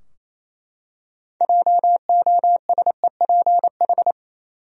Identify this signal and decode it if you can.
Morse code, heard as a keyed tone, about 27 wpm, 700 Hz. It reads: JOSEPH
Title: USA Common Names - 28wpm